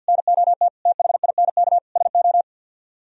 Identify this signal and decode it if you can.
Morse code, heard as a keyed tone, about 64 wpm, 700 Hz. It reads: NOT THINK SO